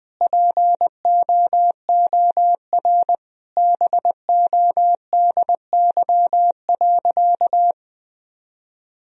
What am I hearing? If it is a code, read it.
Morse code, heard as a keyed tone, about 20 wpm, 700 Hz. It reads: POOR BODY.
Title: Sentences from Top 1000 Words - Review Part 2 - 20wpm